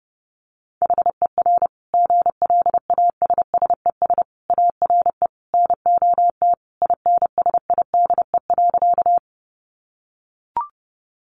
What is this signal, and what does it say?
Morse code, heard as a keyed tone, about 30 wpm, 700 Hz. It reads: HER GLASSES ARE NOT INSIDE.
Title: Additional Easy Phrases 30wpm